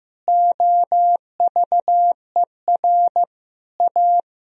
Morse code, heard as a keyed tone, about 15 wpm, 700 Hz. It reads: OVER A
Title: Sentences from Top 1000 Words - Review Part 1 - 15wpm